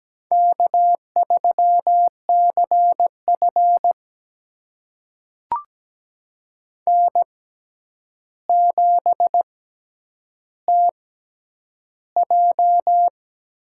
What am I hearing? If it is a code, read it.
Morse code, heard as a keyed tone, about 17 wpm, spaced as for 9 wpm, 700 Hz. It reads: K3CF N7TJ
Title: USA Call Signs - 6x Character Spacing - 17wpm